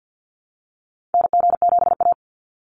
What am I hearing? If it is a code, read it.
Morse code, heard as a keyed tone, about 48 wpm, 700 Hz. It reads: NG7A
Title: NA SOTA Activator Callsigns 50wpm